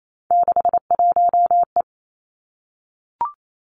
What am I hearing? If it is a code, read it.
Morse code, heard as a keyed tone, about 28 wpm, 700 Hz. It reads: 61E
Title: Three Challenging Letter-Number Combinations 28wpm